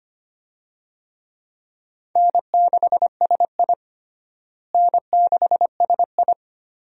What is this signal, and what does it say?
Morse code, heard as a keyed tone, about 25 wpm, 700 Hz. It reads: N6SI N6SI
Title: USA Call Signs - 4 Characters - Repeated 3x - 25wpm